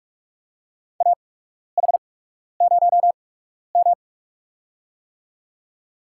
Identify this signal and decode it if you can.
Morse code, heard as a keyed tone, about 48 wpm, spaced as for 12 wpm, 700 Hz. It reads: AH0M
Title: USA Call Signs - 4 Characters - 8x Character Spacing - 45wpm